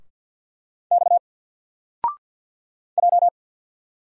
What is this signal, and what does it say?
Morse code, heard as a keyed tone, about 50 wpm, 700 Hz. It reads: X J
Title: Single Character (Letter:Number:Punctuation:Prosign) 50wpm